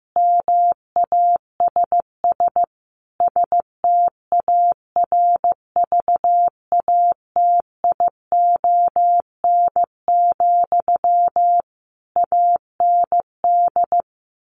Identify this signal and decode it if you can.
Morse code, heard as a keyed tone, about 15 wpm, 700 Hz. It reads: MASS STARVATION, AND
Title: A Sound Of Thunder 15wpm